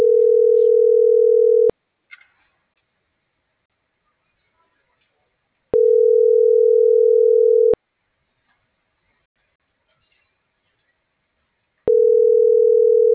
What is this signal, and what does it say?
Ringback tone (North America), 440 + 480 Hz.